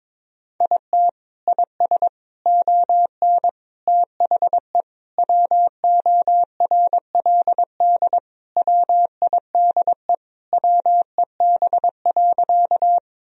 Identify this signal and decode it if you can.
Morse code, heard as a keyed tone, about 22 wpm, 700 Hz. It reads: IT IS ON THE WORLD WIDE WEB.
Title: Sentences from Top 1000 Words - Review Part 4 - 22wpm